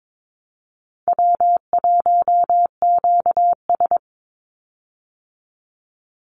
Morse code, heard as a keyed tone, about 22 wpm, 700 Hz. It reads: W1QS
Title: Easy Call Signs Repeated 2x - 22wpm